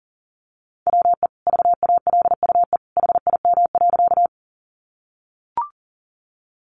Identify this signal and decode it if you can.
Morse code, heard as a keyed tone, about 40 wpm, 700 Hz. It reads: WE VALUE HIM.
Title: Sentences from Top 600 Words 40wpm